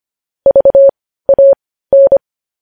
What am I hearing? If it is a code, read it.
Morse code, heard as a keyed tone, about 25 wpm, spaced as for 15 wpm, 550 Hz. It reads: VAN
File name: podcast / News headlines-1x-25x15